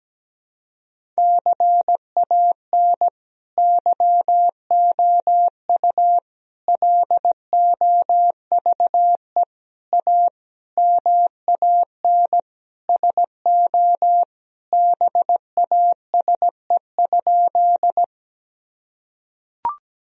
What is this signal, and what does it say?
Morse code, heard as a keyed tone, about 17 wpm, 700 Hz. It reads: CAN YOU LOVE A MAN SO BASE?
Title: Sentences from Top 1000 Words - Review Part 1 - 17wpm